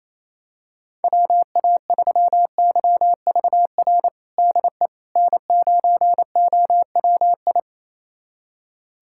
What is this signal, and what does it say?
Morse code, heard as a keyed tone, about 28 wpm, 700 Hz. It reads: WA3YVR DE N9OWI
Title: USA Callsign DE Callsign 28wpm